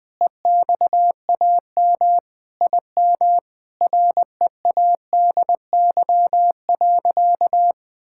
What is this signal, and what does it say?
Morse code, heard as a keyed tone, about 20 wpm, 700 Hz. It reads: EXAM IM READY.
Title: Sentences with 5-Word Phrases - 20wpm